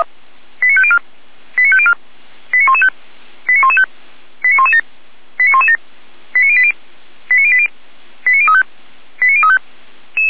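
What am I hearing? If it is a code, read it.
Five-tone selective call (ZVEI), heard as quick runs of five tones, tones of about 70 ms. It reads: 89504 89504 89115 89115 89117 89117 89080 89080 89035 89035
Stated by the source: ZVEI five-tone selective call received by SDR.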